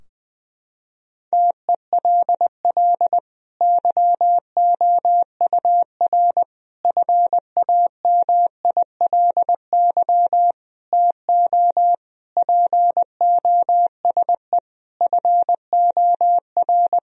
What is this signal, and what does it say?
Morse code, heard as a keyed tone, about 20 wpm, 700 Hz. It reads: TELL YOUR FAMILY TO POSE FOR
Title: Sentences from Top 1000 Words - Review Part 2 - 20wpm